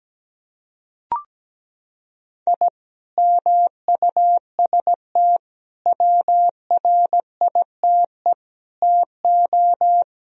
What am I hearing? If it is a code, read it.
Morse code, heard as a keyed tone, about 17 wpm, 700 Hz. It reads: I MUST WRITE TO
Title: Sentences from Top 1000 Words - Review Part 1 - 17wpm